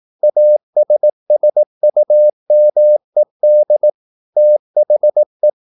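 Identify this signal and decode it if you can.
Morse code, heard as a keyed tone, about 18 wpm, 600 Hz. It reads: ASSUMED THE
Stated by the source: Morse practice at 18 wpm, English text, from CW med Gnister.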